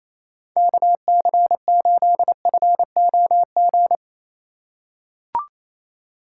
Morse code, heard as a keyed tone, about 28 wpm, 700 Hz. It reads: KC8FOG
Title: USA Call Signs - 2x Character Spacing - 28wpm